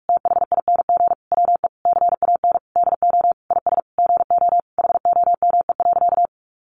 Morse code, heard as a keyed tone, about 44 wpm, 700 Hz. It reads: THING WE CAN DO IS GO HOME.
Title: Sentences from Top 200 Words 45wpm